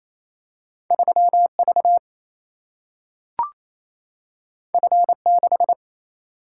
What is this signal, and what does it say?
Morse code, heard as a keyed tone, about 28 wpm, 700 Hz. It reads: 3V F6
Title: Single-Letter Single-Number 28wpm